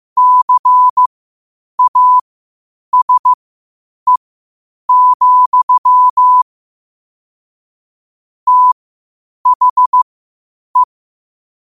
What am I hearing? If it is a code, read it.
Morse code, heard as a keyed tone, about 15 wpm, spaced as for 8 wpm, 1000 Hz. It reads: CASE, THE
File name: qotdfiles / 8wpm